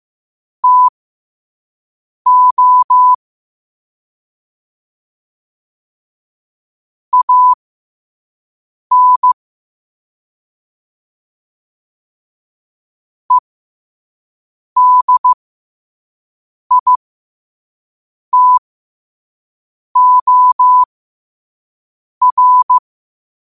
Morse code, heard as a keyed tone, about 15 wpm, spaced as for 5 wpm, 1000 Hz. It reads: TO AN EDITOR
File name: qotdfiles / 5wpm